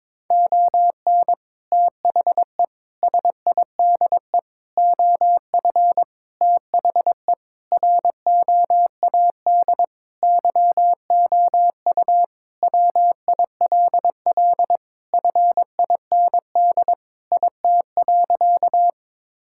Morse code, heard as a keyed tone, about 22 wpm, 700 Hz. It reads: ON THE SIDE OF THE ROAD YOU WILL FIND IT.
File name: practice / 22wpm / Sentences with 3-Word Phrases - 22wpm